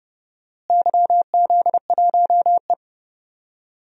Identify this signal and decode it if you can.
Morse code, heard as a keyed tone, about 31 wpm, 700 Hz. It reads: YZ1E
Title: DXCC-100 Call Signs - 4 Characters - Repeated 2x - 30wpm